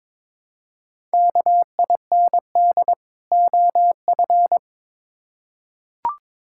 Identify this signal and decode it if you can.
Morse code, heard as a keyed tone, about 21 wpm, 700 Hz. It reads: KIND OF
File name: Build A Phrase - 2-Word Phrases - 22wpm